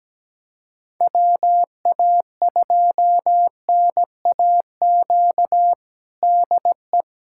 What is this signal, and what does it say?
Morse code, heard as a keyed tone, about 17 wpm, 700 Hz. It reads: WA2NAQ DE
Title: USA Callsign DE Callsign - 6 Characters - 17wpm